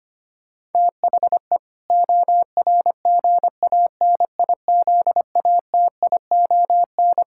Morse code, heard as a keyed tone, about 25 wpm, 700 Hz. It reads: THE ORGANIZATION
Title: Sentences from Top 1000 Words - Review Part 6 - 25wpm